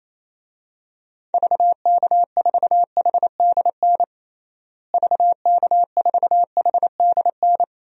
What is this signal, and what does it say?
Morse code, heard as a keyed tone, about 28 wpm, 700 Hz. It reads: VK4HDN VK4HDN
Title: DXCC-100 Call Signs - 6 Characters - Repeated 3x - 28wpm